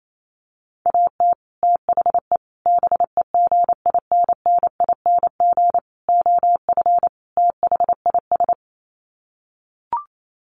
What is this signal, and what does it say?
Morse code, heard as a keyed tone, about 28 wpm, 700 Hz. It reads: AT THE BEGINNING OF THIS
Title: Sets of 5 Words Encore 28wpm